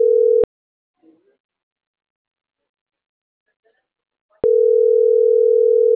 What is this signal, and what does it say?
Ringback tone (North America), 440 + 480 Hz.